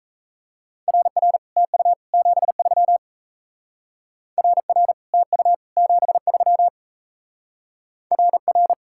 Morse code, heard as a keyed tone, about 37 wpm, 700 Hz. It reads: RR TU 73 RR TU 73 RR